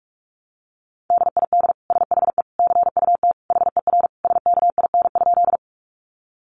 Morse code, heard as a keyed tone, about 44 wpm, 700 Hz. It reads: DID SHE CUT HER SKIN?
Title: Sentences from Top 1300 Words 45wpm